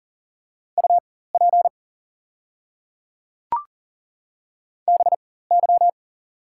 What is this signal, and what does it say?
Morse code, heard as a keyed tone, about 40 wpm, spaced as for 19 wpm, 700 Hz. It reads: UP BY
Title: 4x Character Spacing - Top 100 Words - 40wpm